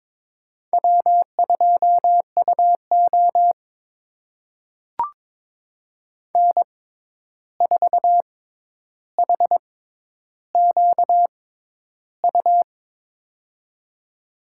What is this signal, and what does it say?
Morse code, heard as a keyed tone, about 22 wpm, spaced as for 11 wpm, 700 Hz. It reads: W2UO N4HQU
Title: USA Call Signs - 6x Character Spacing - 22wpm